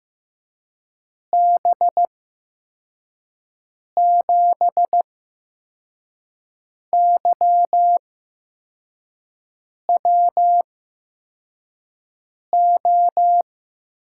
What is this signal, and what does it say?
Morse code, heard as a keyed tone, about 15 wpm, spaced as for 4 wpm, 700 Hz. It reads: B7YWO
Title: USA Call Signs - 8x Character Spacing - 15wpm